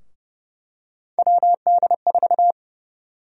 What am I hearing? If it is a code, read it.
Morse code, heard as a keyed tone, about 30 wpm, 700 Hz. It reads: WD4
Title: USA Call Sign Prefixes - 30wpm